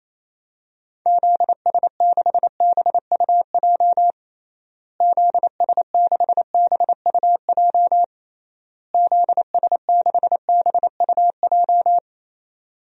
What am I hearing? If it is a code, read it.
Morse code, heard as a keyed tone, about 28 wpm, 700 Hz. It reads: ZS6BUJ ZS6BUJ ZS6BUJ
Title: DXCC-100 Call Signs - 6 Characters - Repeated 3x - 28wpm